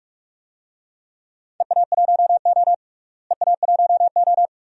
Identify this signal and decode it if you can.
Morse code, heard as a keyed tone, about 42 wpm, 700 Hz. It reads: EA1O EA1O
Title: DXCC-100 Call Signs - 4 Characters - Repeated 2x - 45wpm